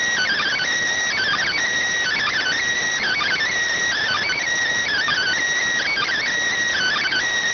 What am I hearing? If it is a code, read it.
Emergency Alert System (SAME) tones (header bursts), heard as harsh buzzing bursts. It is heard from 1.1 to 7.3 s.